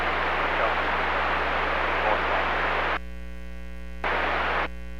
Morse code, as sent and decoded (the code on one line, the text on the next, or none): none